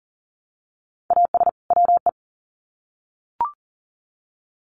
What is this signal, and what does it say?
Morse code, heard as a keyed tone, about 40 wpm, 700 Hz. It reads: AS WE
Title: Sets of 2 Words 40wpm